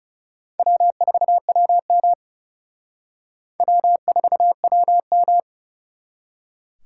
Morse code, heard as a keyed tone, about 37 wpm, 700 Hz. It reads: W4WM W4WM
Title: Speed Racing - USA 1x2 Call Signs 30wpm